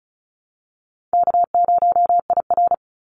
Morse code, heard as a keyed tone, about 34 wpm, 700 Hz. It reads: K0IR
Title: Easy Call Signs Repeated 2x - 35wpm